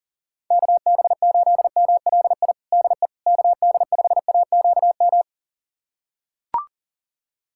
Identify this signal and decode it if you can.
Morse code, heard as a keyed tone, about 40 wpm, 700 Hz. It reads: KB8MLI DE KD5AQM
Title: USA Callsign DE Callsign 40wpm